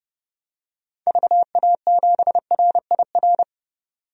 Morse code, heard as a keyed tone, about 30 wpm, 700 Hz. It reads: VA7RIR
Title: DXCC-100 Call Signs - 6 Characters - 30wpm